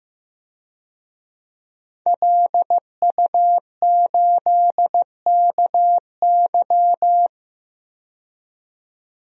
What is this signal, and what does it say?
Morse code, heard as a keyed tone, about 15 wpm, 700 Hz. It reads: LU8KY
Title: DXCC-100 Call Signs - 5 Characters - 15wpm